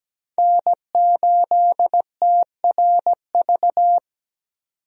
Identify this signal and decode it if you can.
Morse code, heard as a keyed tone, about 17 wpm, 700 Hz. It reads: N8TRV
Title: USA Call Signs - 5 Characters - 17wpm